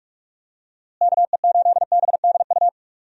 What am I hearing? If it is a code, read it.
Morse code, heard as a keyed tone, about 43 wpm, 700 Hz. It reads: KE8BDU
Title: USA Call Signs - 6 Characters - Repeated 3x - 45wpm